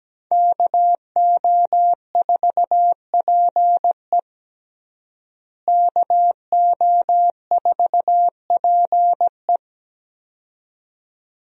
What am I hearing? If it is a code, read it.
Morse code, heard as a keyed tone, about 17 wpm, 700 Hz. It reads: KO4PE KO4PE
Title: USA Call Signs - 5 Characters - Repeated 2x - 17wpm